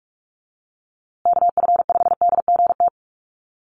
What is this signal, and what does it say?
Morse code, heard as a keyed tone, about 44 wpm, 700 Hz. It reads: KF5DGT
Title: USA Call Signs Repeated 3x 45wpm